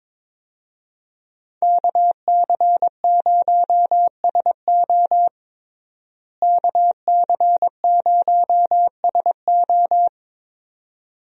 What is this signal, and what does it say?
Morse code, heard as a keyed tone, about 22 wpm, 700 Hz. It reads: KC0SO KC0SO
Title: USA Call Signs - 5 Characters - Repeated 2x - 22wpm